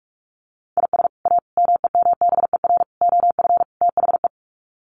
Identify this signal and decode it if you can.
Morse code, heard as a keyed tone, about 44 wpm, 700 Hz. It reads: IS A MEMBER OF THE
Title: Sets of 5 Words 45wpm